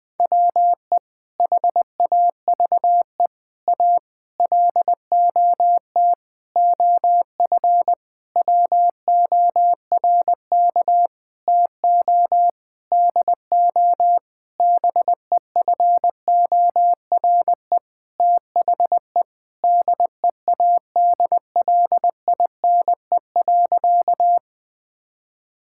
Morse code, heard as a keyed tone, about 20 wpm, 700 Hz. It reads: WE HAVE A LOT OF WORK TO DO BEFORE THE DEADLINE.
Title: Sentences with 8-Word Phrases - 20wpm